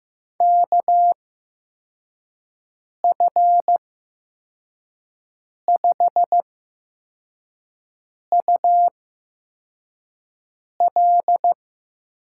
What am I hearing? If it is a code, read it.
Morse code, heard as a keyed tone, about 15 wpm, spaced as for 4 wpm, 700 Hz. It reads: KF5UL
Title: USA Call Signs - 6 Characters - 8x Character Spacing - 15wpm